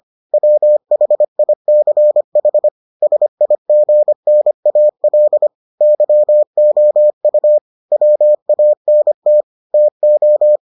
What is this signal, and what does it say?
Morse code, heard as a keyed tone, about 25 wpm, 600 Hz. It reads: WHICH SIGNAL YOU WANT TO